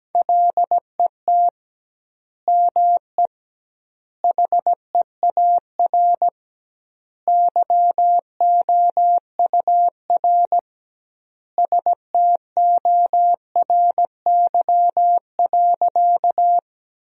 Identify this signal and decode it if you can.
Morse code, heard as a keyed tone, about 17 wpm, 700 Hz. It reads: LET ME HEAR YOUR STORY.
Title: 2x Word-Spacing - Sentences from Top 300 Words - 17wpm